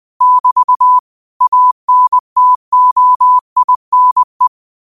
Morse code, heard as a keyed tone, about 20 wpm, 1000 Hz. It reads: <BT> ANTOINE
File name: qotdfiles / 20wpm